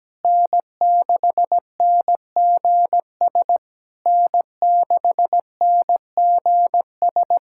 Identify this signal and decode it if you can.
Morse code, heard as a keyed tone, about 17 wpm, 700 Hz. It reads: N6NGS N6NGS
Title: SOTA QSO Practice 17wpm